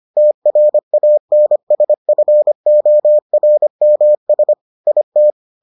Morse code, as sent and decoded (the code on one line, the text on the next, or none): - .-. .- -. ... ..-. --- .-. -- ... / .. -
TRANSFORMS IT